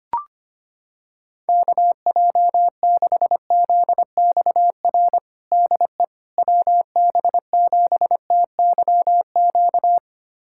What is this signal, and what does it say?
Morse code, heard as a keyed tone, about 25 wpm, 700 Hz. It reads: KJ6ZXR DE WB7TYQ
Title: USA Callsign DE Callsign - 6 Characters - 25wpm